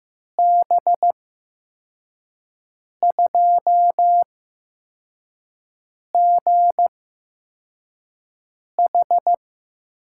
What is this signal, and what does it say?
Morse code, heard as a keyed tone, about 15 wpm, spaced as for 4 wpm, 700 Hz. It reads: B2GH
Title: USA Call Signs - 6 Characters - 8x Character Spacing - 15wpm